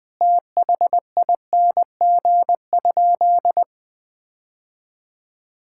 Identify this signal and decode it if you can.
Morse code, heard as a keyed tone, about 21 wpm, 700 Hz. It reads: THING?
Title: Sentences from Top 1000 Words - Review Part 5 - 20wpm